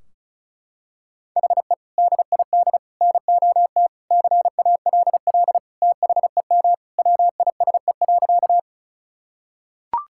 Morse code, heard as a keyed tone, about 36 wpm, 700 Hz. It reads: HE DID NOT CALL THEM WISE.
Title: Sentences from Top 1600 Words 35wpm